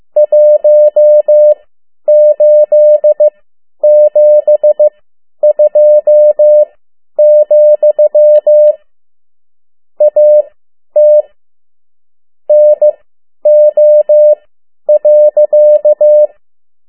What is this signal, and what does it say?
Morse code, heard as a keyed tone, about 15 wpm, spaced as for 10 wpm, 600 Hz. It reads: 1872, AT NO.
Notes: Around the World in 80 Days paragraph one Morse code 10 WPM